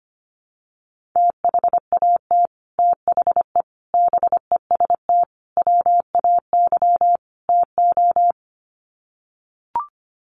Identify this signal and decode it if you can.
Morse code, heard as a keyed tone, about 25 wpm, 700 Hz. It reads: THAT THE BEST WAY TO
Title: Sets of 5 Words Encore2 25wpm